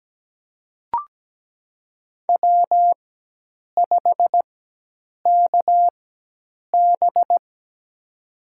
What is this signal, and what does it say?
Morse code, heard as a keyed tone, about 17 wpm, spaced as for 8 wpm, 700 Hz. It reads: W5KB
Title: USA Call Signs - 4x Character Spacing - 17wpm